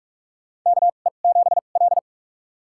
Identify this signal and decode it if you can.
Morse code, heard as a keyed tone, about 46 wpm, spaced as for 32 wpm, 700 Hz. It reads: KE7L
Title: USA Call Signs - 4 Characters - 2x Character Spacing - 45wpm